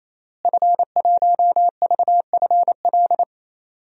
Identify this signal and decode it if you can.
Morse code, heard as a keyed tone, about 28 wpm, 700 Hz. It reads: F1VFL
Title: DXCC-100 Call Signs - 5 Characters - Repeated 3x - 28wpm